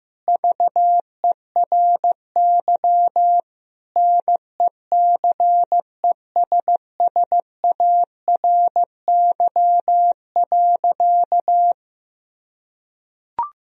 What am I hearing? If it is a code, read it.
Morse code, heard as a keyed tone, about 15 wpm, 700 Hz. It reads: VERY NECESSARY.
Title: Sentences from Top 1000 Words - Review Part 3 - 15wpm